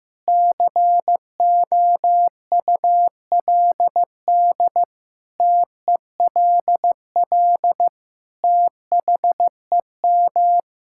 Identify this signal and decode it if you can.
Morse code, heard as a keyed tone, about 15 wpm, 700 Hz. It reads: COULD TELL THEM
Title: Sentences from Top 1000 Words - Review Part 6 - 15wpm